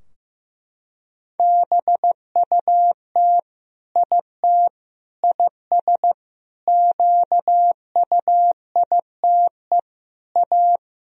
Morse code, heard as a keyed tone, about 15 wpm, 700 Hz. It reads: BUT IT IS QUITE A
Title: Sentences from Top 1000 Words - Review Part 2 - 15wpm